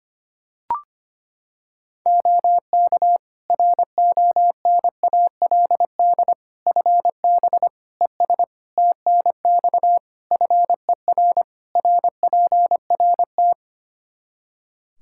Morse code, heard as a keyed tone, about 25 wpm, 700 Hz. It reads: OK RONALD FB ES TNX FER RPRT
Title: Practice QSO - Exchange 2 - Info - 25wpm